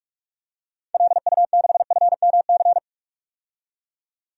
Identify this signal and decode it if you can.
Morse code, heard as a keyed tone, about 46 wpm, 700 Hz. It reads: LU6FMC
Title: DXCC-100 Call Signs - 6 Characters - 45wpm